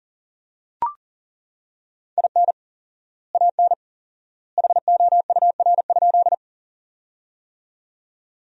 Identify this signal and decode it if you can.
Morse code, heard as a keyed tone, about 40 wpm, 700 Hz. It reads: IN AN HOUR?
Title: 4x Word-Spacing - Sentences from Top 400 Words - 40wpm